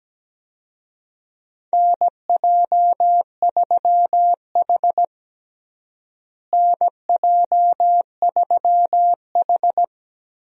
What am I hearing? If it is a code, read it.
Morse code, heard as a keyed tone, about 17 wpm, 700 Hz. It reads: NJ3H NJ3H
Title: USA Call Signs - 4 Characters - Repeated 3x - 17wpm